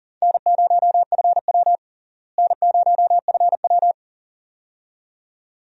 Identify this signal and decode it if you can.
Morse code, heard as a keyed tone, about 40 wpm, 700 Hz. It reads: N0FW N0FW
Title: USA Call Signs - 4 Characters - Repeated 2x - 40wpm